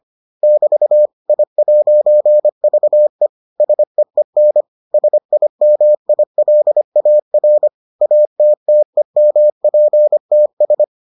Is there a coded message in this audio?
Yes: Morse code from 0.4 to 10.8 s, heard as a keyed tone, about 25 wpm, 600 Hz. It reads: <BT> I'VE SEEN SIMILAR ATTEMPTS